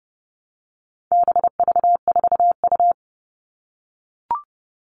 Morse code, heard as a keyed tone, about 30 wpm, 700 Hz. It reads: BV4U
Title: All Cut Numbers - 4-Digit 30wpm